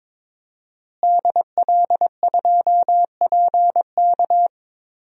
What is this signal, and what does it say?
Morse code, heard as a keyed tone, about 22 wpm, 700 Hz. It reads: DL2PK
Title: DXCC-100 Call Signs - 5 Characters - Repeated 3x - 22wpm